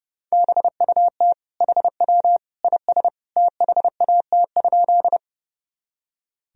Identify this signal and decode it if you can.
Morse code, heard as a keyed tone, about 32 wpm, 700 Hz. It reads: BUT HW IS THAT?
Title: Sentences from Top 100 Words and Abbreviations 30wpm